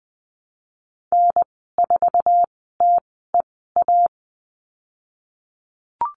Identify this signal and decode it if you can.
Morse code, heard as a keyed tone, about 20 wpm, 700 Hz. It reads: N 4 T E A
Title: 076 Lesson 13g - Letter L - Call Signs 20wpm